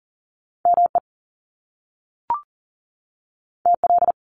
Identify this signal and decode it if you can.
Morse code, heard as a keyed tone, about 40 wpm, 700 Hz. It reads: ME TL
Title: Top Two Letter Combinations in Words 40wpm